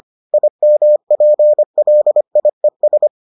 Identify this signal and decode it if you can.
Morse code, heard as a keyed tone, about 25 wpm, 600 Hz. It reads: IMPLIES